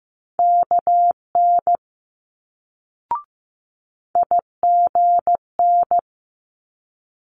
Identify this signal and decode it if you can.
Morse code, heard as a keyed tone, about 15 wpm, 700 Hz. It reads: KN IGN
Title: Multi-letter Phonemes Quick Fire 15wpm